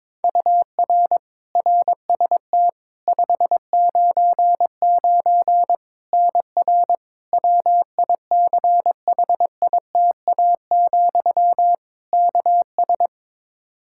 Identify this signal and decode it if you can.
Morse code, heard as a keyed tone, about 22 wpm, 700 Hz. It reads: UR RST 599 NR WICHITA, KS
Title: K1USN SST QSO Practice 22wpm